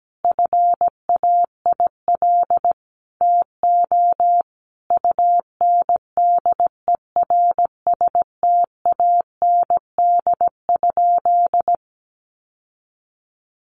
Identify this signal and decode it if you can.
Morse code, heard as a keyed tone, about 17 wpm, 700 Hz. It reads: FAIL TO UNDERSTAND?
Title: Sentences from Top 1100 Words 17wpm